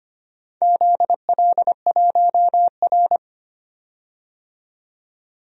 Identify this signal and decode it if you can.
Morse code, heard as a keyed tone, about 25 wpm, 700 Hz. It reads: ZL1R
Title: DXCC-100 Call Signs - 4 Characters - 25wpm